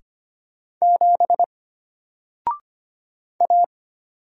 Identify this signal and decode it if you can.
Morse code, heard as a keyed tone, about 25 wpm, 700 Hz. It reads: 7 A
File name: Single-Letter-Number-25wpm